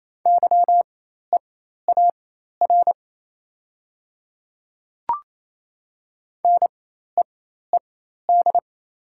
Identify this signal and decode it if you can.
Morse code, heard as a keyed tone, about 27 wpm, spaced as for 13 wpm, 700 Hz. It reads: YEAR NEED
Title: 4x Character Spacing - Top 300 Words - 28wpm